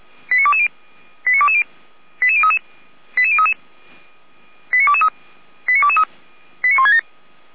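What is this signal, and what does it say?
Five-tone selective call (ZVEI), heard as quick runs of five tones, tones of about 70 ms. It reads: 89220 89220 88033 88033 89223 89223 89167
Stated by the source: ZVEI five-tone selective call 89123 received on SDR, on the Feuerwehr Orlamünde (f85360) audio server.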